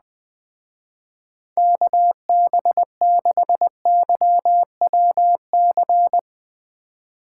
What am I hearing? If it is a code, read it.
Morse code, heard as a keyed tone, about 20 wpm, 700 Hz. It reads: KB6YWC
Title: Speed Racing - USA 2x3 Call Signs 17wpm